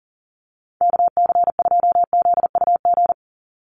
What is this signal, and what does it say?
Morse code, heard as a keyed tone, about 40 wpm, 700 Hz. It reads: KC2ZUG
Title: USA Call Signs 40wpm